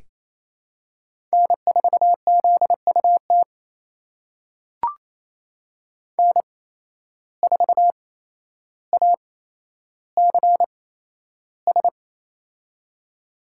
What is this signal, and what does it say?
Morse code, heard as a keyed tone, about 27 wpm, spaced as for 12 wpm, 700 Hz. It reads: N4ZUT N4ACS
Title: USA Call Signs - 5 Characters - 8x Character Spacing - 28wpm